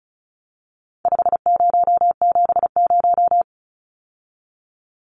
Morse code, heard as a keyed tone, about 34 wpm, 700 Hz. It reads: 5070
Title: Numbers - 4-Digit 35wpm